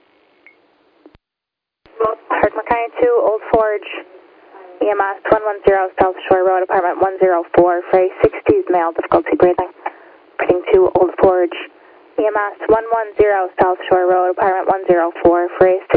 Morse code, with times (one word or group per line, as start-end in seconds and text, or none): none